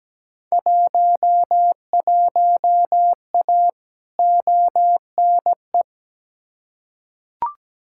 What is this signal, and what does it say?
Morse code, heard as a keyed tone, about 17 wpm, 700 Hz. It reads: 11A ONE
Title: Field Day Exchanges Encore - 17wpm